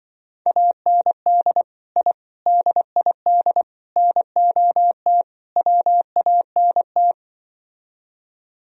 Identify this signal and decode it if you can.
Morse code, heard as a keyed tone, about 23 wpm, 700 Hz. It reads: AND I DID NOT WANT
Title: Speed-Racing - Sets of 5 Words 20wpm